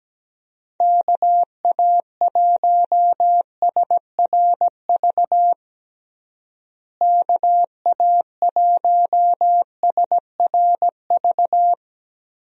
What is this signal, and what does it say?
Morse code, heard as a keyed tone, about 17 wpm, 700 Hz. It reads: KA1SRV KA1SRV
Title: USA Call Signs - 6 Characters - Repeated 3x - 17wpm